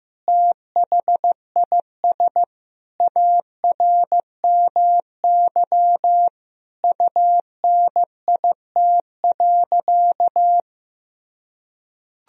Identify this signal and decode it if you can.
Morse code, heard as a keyed tone, about 15 wpm, 700 Hz. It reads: THIS ARMY UNIT.
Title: Sentences from Top 1000 Words - Review Part 2 - 15wpm